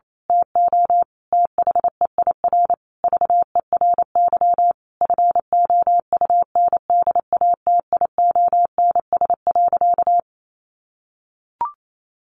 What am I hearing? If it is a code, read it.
Morse code, heard as a keyed tone, about 28 wpm, 700 Hz. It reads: TO THEIR VERY FOUNDATIONS.
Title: A Sound Of Thunder 28wpm